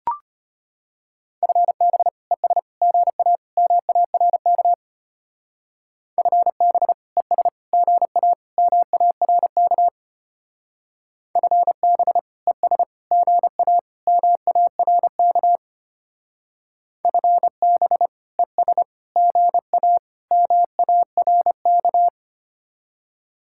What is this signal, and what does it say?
Morse code, heard as a keyed tone, about 30 wpm, 700 Hz. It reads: FB ES GA MARK FB ES GA MARK FB ES GA MARK FB ES GA MARK
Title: Speed-Racing Common QSO Phrases - 25wpm